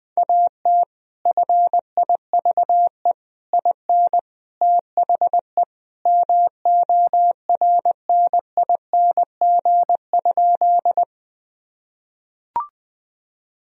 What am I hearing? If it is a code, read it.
Morse code, heard as a keyed tone, about 20 wpm, 700 Hz. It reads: AT FIVE IN THE MORNING?
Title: Sentences from Top 1000 Words - Review Part 2 - 20wpm